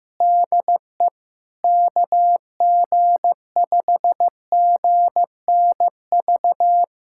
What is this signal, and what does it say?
Morse code, heard as a keyed tone, about 15 wpm, 700 Hz. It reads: DE KG5GNV
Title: SOTA QSO Practice 15wpm